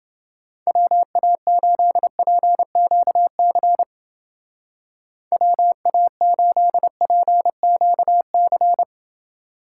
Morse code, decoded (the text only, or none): WA8PQC WA8PQC